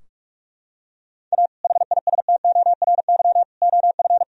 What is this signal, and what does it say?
Morse code, heard as a keyed tone, about 42 wpm, 700 Hz. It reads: A HISTORY OF
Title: Sentences from Top 1000 Words - Review Part 3 - 45wpm